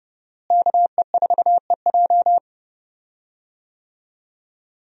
Morse code, heard as a keyed tone, about 32 wpm, 700 Hz. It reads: KE4EJ
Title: USA Call Signs - 5 Characters - Repeated 3x - 30wpm